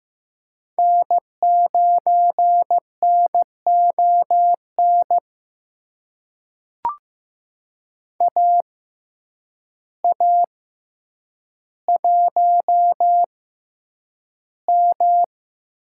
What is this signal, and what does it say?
Morse code, heard as a keyed tone, about 15 wpm, spaced as for 8 wpm, 700 Hz. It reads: N9NON AA1M
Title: USA Call Signs - 5 Characters - 6x Character Spacing - 15wpm